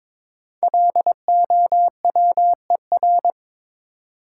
Morse code, heard as a keyed tone, about 23 wpm, 700 Hz. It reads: LOWER
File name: 5 Letter Words - First 3 Letters Given - 22wpm